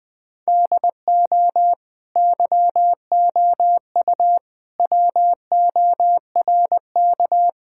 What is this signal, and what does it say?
Morse code, heard as a keyed tone, about 21 wpm, 700 Hz. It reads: DO YOU WORK
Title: Sentences from Top 1000 Words - Review Part 5 - 20wpm